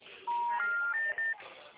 Special information tone, with noise.
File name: out- -2009- - - .33534